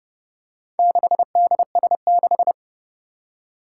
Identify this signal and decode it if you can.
Morse code, heard as a keyed tone, about 30 wpm, 700 Hz. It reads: 6DS6
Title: Four Challenging Letter-Number Combinations 30wpm